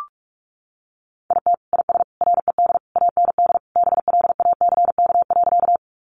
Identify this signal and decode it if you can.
Morse code, heard as a keyed tone, about 44 wpm, 700 Hz. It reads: IT IS RED AND BLACK.
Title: Additional Easy Phrases 45wpm